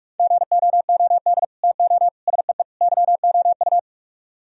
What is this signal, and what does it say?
Morse code, heard as a keyed tone, about 38 wpm, 700 Hz. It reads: GOOD TO SEE YOU